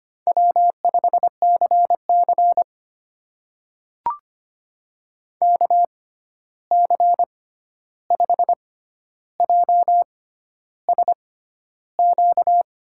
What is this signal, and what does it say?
Morse code, heard as a keyed tone, about 25 wpm, spaced as for 11 wpm, 700 Hz. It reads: W5CC KC5JSQ
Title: USA Call Signs - 6x Character Spacing - 25wpm